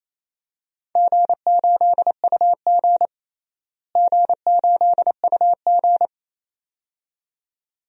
Morse code, heard as a keyed tone, about 28 wpm, 700 Hz. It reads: G8UG G8UG
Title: DXCC-100 Call Signs - 4 Characters - Repeated 2x - 28wpm